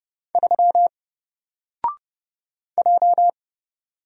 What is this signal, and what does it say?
Morse code, heard as a keyed tone, about 30 wpm, 700 Hz. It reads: 3 J
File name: Single-Letter-Number-30wpm